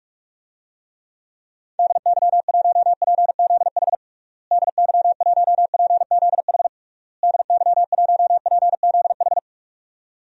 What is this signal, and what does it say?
Morse code, heard as a keyed tone, about 46 wpm, 700 Hz. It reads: DY1PZH DY1PZH DY1PZH
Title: DXCC-100 Call Signs - 6 Characters - Repeated 3x - 45wpm